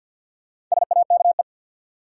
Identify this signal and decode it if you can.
Morse code, heard as a keyed tone, about 50 wpm, 700 Hz. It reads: SAKE
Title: Sets of 4 Words - New Word Drill - 50wpm